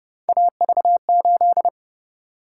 Morse code, heard as a keyed tone, about 29 wpm, 700 Hz. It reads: AV8